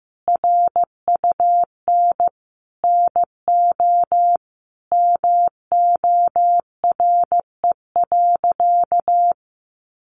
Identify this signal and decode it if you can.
Morse code, heard as a keyed tone, about 15 wpm, 700 Hz. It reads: RUN NO MORE.
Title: Sentences from Top 300 Words 15wpm